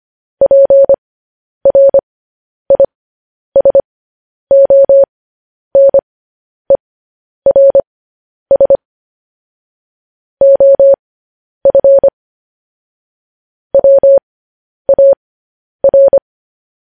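Morse code, heard as a keyed tone, about 25 wpm, spaced as for 10 wpm, 550 Hz. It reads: PRISONERS OF WAR
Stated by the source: Morse code news headlines sent at 10.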